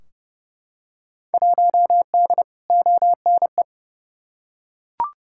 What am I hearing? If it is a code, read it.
Morse code, heard as a keyed tone, about 29 wpm, 700 Hz. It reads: 1D ONE
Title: Field Day Exchanges - 30wpm